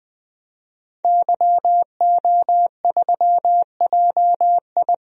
Morse code, heard as a keyed tone, about 21 wpm, 700 Hz. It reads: YO3JI
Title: DXCC-100 Call Signs - 5 Characters - 20wpm